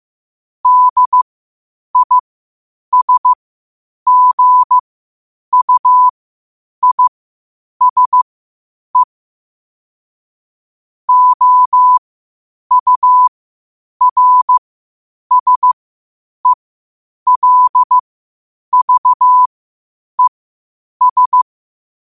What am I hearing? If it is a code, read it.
Morse code, heard as a keyed tone, about 15 wpm, spaced as for 8 wpm, 1000 Hz. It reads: DISGUISE OURSELVES
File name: qotdfiles / 8wpm